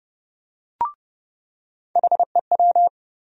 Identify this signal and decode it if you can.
Morse code, heard as a keyed tone, about 30 wpm, 700 Hz. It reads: HEW
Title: USA Call Sign Postfixes - 3 Letters 30wpm